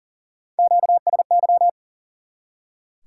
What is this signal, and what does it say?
Morse code, heard as a keyed tone, about 39 wpm, 700 Hz. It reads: QSY
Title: Top 180 QSO Elements 40wpm